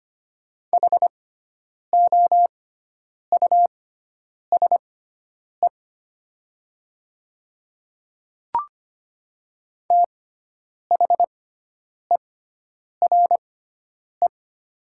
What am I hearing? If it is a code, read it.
Morse code, heard as a keyed tone, about 25 wpm, spaced as for 9 wpm, 700 Hz. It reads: HOUSE THERE
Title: 6x Character Spacing - Top 500 Words - 25wpm